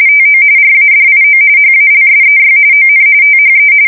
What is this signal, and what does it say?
RTTY, about 45.45 baud. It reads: TO RADIO HALSEY ON THE
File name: fsk170 / battleshipnj